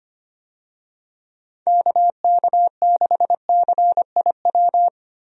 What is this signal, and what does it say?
Morse code, heard as a keyed tone, about 25 wpm, 700 Hz. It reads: KK6CIW
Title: USA Call Signs - 6 Characters - Repeated 2x - 25wpm